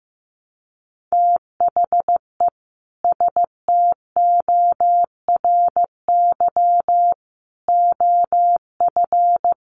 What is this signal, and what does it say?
Morse code, heard as a keyed tone, about 15 wpm, 700 Hz. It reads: THE STORY OF
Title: Sets of 3 Words 15wpm